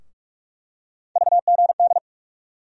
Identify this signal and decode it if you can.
Morse code, heard as a keyed tone, about 44 wpm, 700 Hz. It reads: VGD
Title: USA Call Sign Postfixes - 3 Letters 45wpm